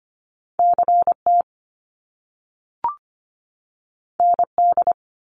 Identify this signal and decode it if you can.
Morse code, heard as a keyed tone, about 25 wpm, 700 Hz. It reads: CT ND
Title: US States 25wpm